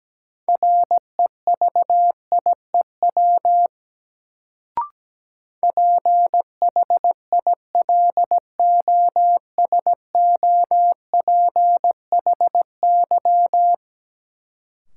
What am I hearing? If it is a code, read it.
Morse code, heard as a keyed tone, about 17 wpm, 700 Hz. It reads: REVIEW PHILOSOPHY
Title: Sentences from Top 900 Words - New Word Drill - 17wpm